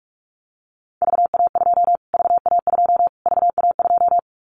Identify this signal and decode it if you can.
Morse code, heard as a keyed tone, about 44 wpm, 700 Hz. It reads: VA2 VA2 VA2
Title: Callsign Prefixes for Top 25 DXCC - Repeated 3x - 45wpm